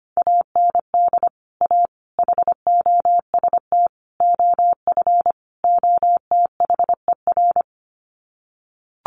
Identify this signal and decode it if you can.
Morse code, heard as a keyed tone, about 25 wpm, 700 Hz. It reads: AND A HOST OF OTHER
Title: Sets of 5 Words Encore2 25wpm